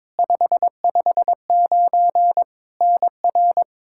Morse code, heard as a keyed tone, about 22 wpm, 700 Hz. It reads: 559 NR
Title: Practice QSO - Exchange 1 - Beginning - 22wpm